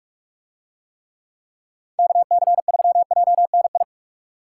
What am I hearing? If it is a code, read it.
Morse code, heard as a keyed tone, about 40 wpm, 700 Hz. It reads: KC3JNI